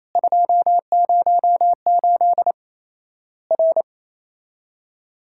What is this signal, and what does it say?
Morse code, heard as a keyed tone, about 28 wpm, 700 Hz. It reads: 208 R
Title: Worked All Europe DX Contest - QTC Exchange - 28wpm